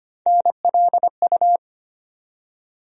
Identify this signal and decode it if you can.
Morse code, heard as a keyed tone, about 25 wpm, 700 Hz. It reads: NLU
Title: USA Call Sign Postfixes - 3 Letters 25wpm